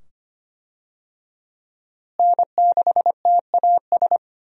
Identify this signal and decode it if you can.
Morse code, heard as a keyed tone, about 25 wpm, 700 Hz. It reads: N6TAS
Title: USA Call Signs - 5 Characters - Repeated 2x - 25wpm